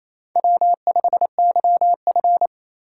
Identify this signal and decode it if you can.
Morse code, heard as a keyed tone, about 36 wpm, 700 Hz. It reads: W5YF